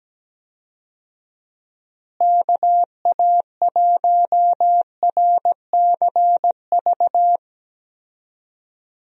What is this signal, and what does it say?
Morse code, heard as a keyed tone, about 17 wpm, 700 Hz. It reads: KA1RCV
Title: USA Call Signs - 6 Characters - Repeated 2x - 17wpm